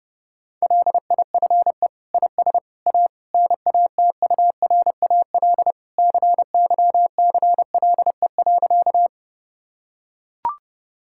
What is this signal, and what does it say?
Morse code, heard as a keyed tone, about 30 wpm, 700 Hz. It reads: LIFE IS A NATURAL CYCLE.
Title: Sentences from Top 1000 Words - Review Part 6 - 30wpm